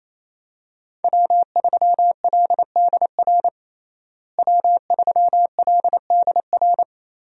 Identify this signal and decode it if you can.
Morse code, heard as a keyed tone, about 27 wpm, 700 Hz. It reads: W3LDR W3LDR
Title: USA Call Signs - 5 Characters - Repeated 3x - 28wpm